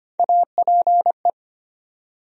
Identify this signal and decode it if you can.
Morse code, heard as a keyed tone, about 25 wpm, 700 Hz. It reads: APE
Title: USA Call Sign Postfixes - 3 Letters 25wpm